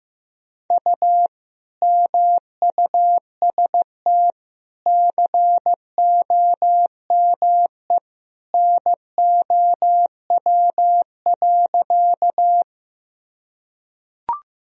Morse code, heard as a keyed tone, about 15 wpm, 700 Hz. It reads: U MUST COME NOW.
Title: Sentences from Top 200 Words and Abbreviations 15wpm